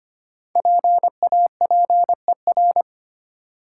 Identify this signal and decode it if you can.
Morse code, heard as a keyed tone, about 25 wpm, 700 Hz. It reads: PAPER
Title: Five Letter Words - 25wpm